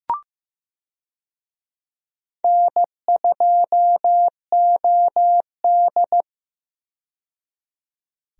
Morse code, heard as a keyed tone, about 15 wpm, 700 Hz. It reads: N2OD
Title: USA Call Signs - 4 Characters - 15wpm